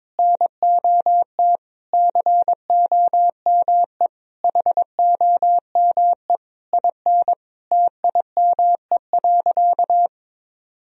Morse code, heard as a keyed tone, about 21 wpm, 700 Hz. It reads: NOT COME HOME IN TIME.
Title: Sentences from Top 1000 Words - Review Part 3 - 22wpm